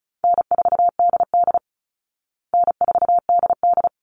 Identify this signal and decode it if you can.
Morse code, heard as a keyed tone, about 34 wpm, 700 Hz. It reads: N4DD N4DD
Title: Call Signs Repeated 3x - 35wpm